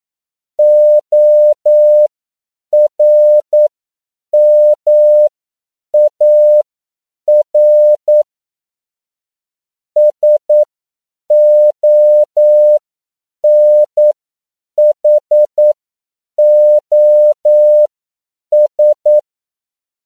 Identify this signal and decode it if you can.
Morse code, heard as a keyed tone, about 9 wpm, spaced as for 7 wpm, 600 Hz. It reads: ORMAR SONHOS
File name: CW / 10PPM